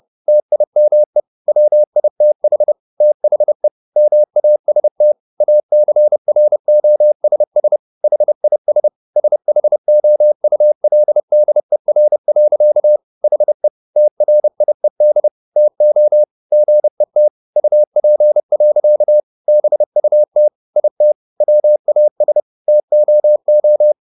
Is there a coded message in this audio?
Yes: Morse code from 0.3 to 23.9 s, heard as a keyed tone, about 30 wpm, 600 Hz. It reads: TIME WITH THE MAST ACROSS HIS SHOULDER. HE TRIED TO GET UP. BUT IT WAS TOO